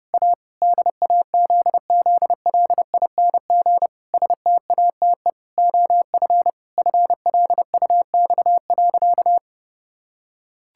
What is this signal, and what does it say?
Morse code, heard as a keyed tone, about 30 wpm, 700 Hz. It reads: A DAZZLING STATE OF FLUX.
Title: Pangram Challenge v2 - 30wpm